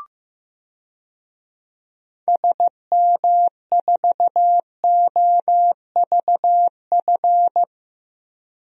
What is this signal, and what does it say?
Morse code, heard as a keyed tone, about 15 wpm, 700 Hz. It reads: SM4OVF
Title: DXCC-100 Call Signs - 6 Characters - 15wpm